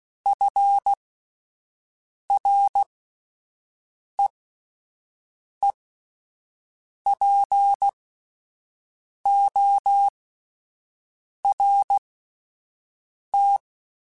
Morse code, heard as a keyed tone, about 16 wpm, spaced as for 5 wpm, 800 Hz. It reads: FREEPORT